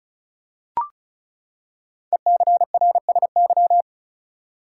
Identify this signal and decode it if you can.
Morse code, heard as a keyed tone, about 36 wpm, 700 Hz. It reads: ECRSY
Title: Code Groups - 5 Letters - 35wpm